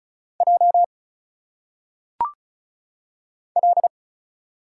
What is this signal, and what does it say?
Morse code, heard as a keyed tone, about 36 wpm, 700 Hz. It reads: J L
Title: Single Letters - Phonetic Pronunciation 35wpm